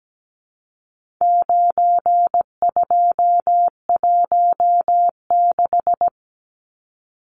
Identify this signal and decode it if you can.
Morse code, heard as a keyed tone, about 17 wpm, 700 Hz. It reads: 9216
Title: Numbers - 4-Digit 17wpm